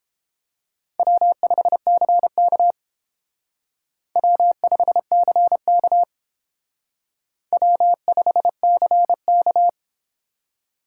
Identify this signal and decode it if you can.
Morse code, heard as a keyed tone, about 30 wpm, 700 Hz. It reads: W5CK W5CK W5CK